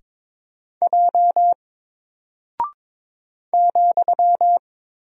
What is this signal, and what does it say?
Morse code, heard as a keyed tone, about 22 wpm, 700 Hz. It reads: J ,
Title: Single Character (Letter:Number:Punctuation:Prosign) Alt 22wpm